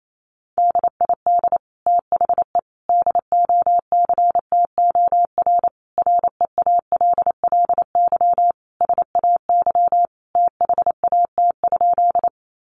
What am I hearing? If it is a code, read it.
Morse code, heard as a keyed tone, about 28 wpm, 700 Hz. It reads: DID THE DOCTOR REALLY SAY THAT?
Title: Sentences from Top 1500 Words 28wpm